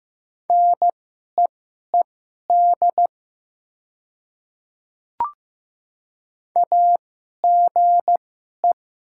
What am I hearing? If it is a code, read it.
Morse code, heard as a keyed tone, about 15 wpm, spaced as for 11 wpm, 700 Hz. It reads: NEED AGE
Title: 2x Character Spacing - Top 500 Words - 15wpm